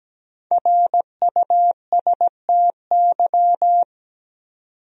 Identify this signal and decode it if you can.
Morse code, heard as a keyed tone, about 17 wpm, 700 Hz. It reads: RUSTY
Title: USA Common Names - Max 5 Letters - 17wpm